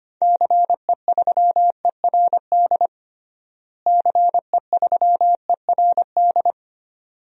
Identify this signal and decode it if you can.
Morse code, heard as a keyed tone, about 25 wpm, 700 Hz. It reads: CE3ERD CE3ERD
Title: DXCC-100 Call Signs - 6 Characters - Repeated 2x - 25wpm